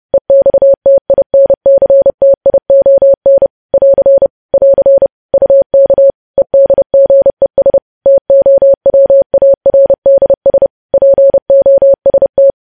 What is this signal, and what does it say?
Morse code, heard as a keyed tone, about 30 wpm, 550 Hz. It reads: EXTINCTION <AR> <AR> UK EDGES TOWARDS POST
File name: podcast / News headlines-1x-30x30